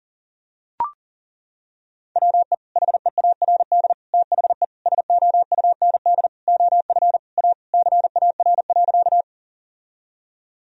Morse code, heard as a keyed tone, about 37 wpm, 700 Hz. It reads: WE HEARD THE SOUND OF A CAR.
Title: Sentences from Top 1000 Words - Review Part 1 - 40wpm